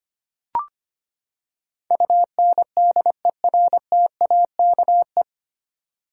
Morse code, heard as a keyed tone, about 25 wpm, 700 Hz. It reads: UNDERTAKE
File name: Nine Letter Words - 25wpm